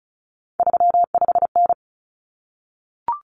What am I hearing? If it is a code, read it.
Morse code, heard as a keyed tone, about 34 wpm, 700 Hz. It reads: 35N
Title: Common Cut Numbers - 3-Digit 35wpm